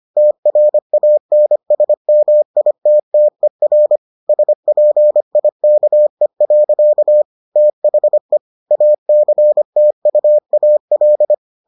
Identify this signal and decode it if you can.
Morse code, heard as a keyed tone, about 25 wpm, 600 Hz. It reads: TRANSMITTER SPIKE. THE ACTUAL